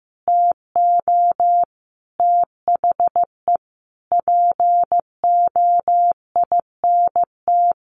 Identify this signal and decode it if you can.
Morse code, heard as a keyed tone, about 15 wpm, 700 Hz. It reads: TO THE POINT
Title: Sentences from Top 900 Words 15wpm